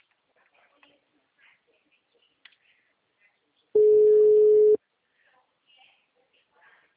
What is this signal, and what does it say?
Ringback tone (Europe, 425 Hz), with speech.
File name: out- -2036- - - .64246